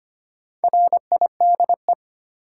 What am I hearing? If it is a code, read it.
Morse code, heard as a keyed tone, about 25 wpm, 700 Hz. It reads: RIDE
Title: Sentences from Top 1200 Words - New Word Drill Alt - 25wpm